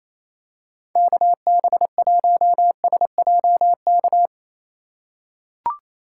Morse code, heard as a keyed tone, about 28 wpm, 700 Hz. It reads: KB1SJK
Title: USA Call Signs - 6 Characters - 4x Character Spacing - 28wpm